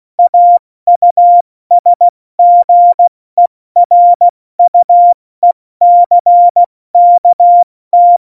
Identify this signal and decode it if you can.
Morse code, heard as a keyed tone, about 16 wpm, spaced as for 14 wpm, 700 Hz. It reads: AUSGERUECKT
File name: Morsekurs / Witze / Witz 27. Februar 2018, 70 F80